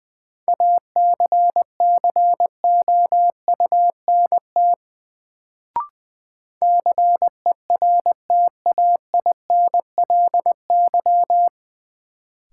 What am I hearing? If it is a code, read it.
Morse code, heard as a keyed tone, about 20 wpm, 700 Hz. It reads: ACCOUNT CERTAINLY
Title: Sentences from Top 900 Words - New Word Drill - 20wpm